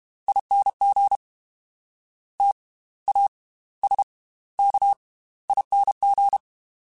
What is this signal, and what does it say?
Morse code, heard as a keyed tone, about 32 wpm, spaced as for 18 wpm, 800 Hz. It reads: ING TASKING